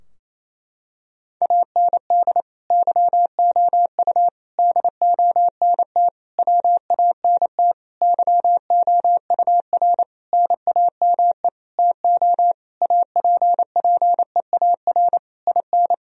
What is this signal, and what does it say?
Morse code, heard as a keyed tone, about 28 wpm, 700 Hz. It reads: AND YOU DONT WANT YOUR NAME TO APPEAR IN
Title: Sentences from Top 1000 Words - Review Part 2 - 28wpm